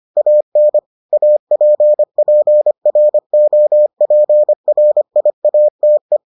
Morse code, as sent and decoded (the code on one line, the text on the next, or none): .- -. / .- .--. .--. .-. --- .--. .-. .. .- - .
AN APPROPRIATE